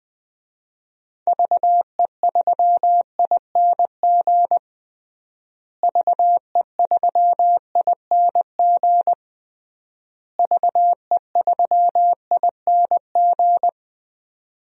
Morse code, heard as a keyed tone, about 20 wpm, 700 Hz. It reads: VE3ING VE3ING VE3ING
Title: DXCC-100 Call Signs - 6 Characters - Repeated 3x - 20wpm